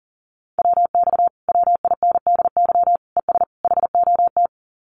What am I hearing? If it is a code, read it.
Morse code, heard as a keyed tone, about 40 wpm, 700 Hz. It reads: WX WINDY ES HOT
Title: Common QSO Phrases Encore 40wpm